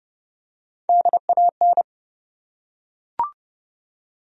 Morse code, heard as a keyed tone, about 32 wpm, 700 Hz. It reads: DAN
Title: USA Common Names - Max 5 Letters - 30wpm